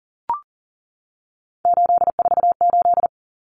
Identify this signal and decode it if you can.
Morse code, heard as a keyed tone, about 40 wpm, 700 Hz. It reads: 848
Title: Common Cut Numbers - 3-Digit 40wpm